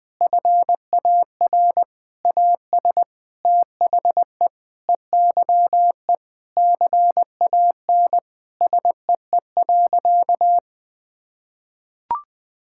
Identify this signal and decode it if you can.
Morse code, heard as a keyed tone, about 20 wpm, 700 Hz. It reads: FAR AS THE EYE CAN SEE.
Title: Sentences with 7-Word Phrases - 20wpm